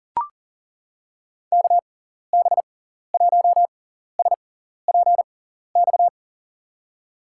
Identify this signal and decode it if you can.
Morse code, heard as a keyed tone, about 40 wpm, spaced as for 14 wpm, 700 Hz. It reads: KB1SPX
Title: USA Call Signs - 6x Character Spacing - 40wpm